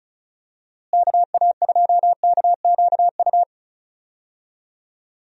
Morse code, heard as a keyed tone, about 34 wpm, 700 Hz. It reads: KA2KQU
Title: USA Call Signs - 6 Characters - Repeated 2x - 35wpm